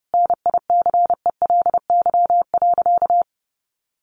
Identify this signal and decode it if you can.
Morse code, heard as a keyed tone, about 30 wpm, 700 Hz. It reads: NICELY.
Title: Easy Phrases 30wpm